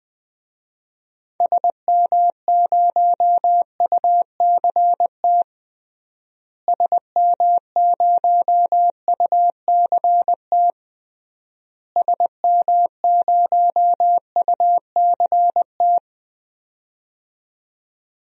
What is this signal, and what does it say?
Morse code, heard as a keyed tone, about 20 wpm, 700 Hz. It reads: SM0UCT SM0UCT SM0UCT
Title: DXCC-100 Call Signs - 6 Characters - Repeated 3x - 20wpm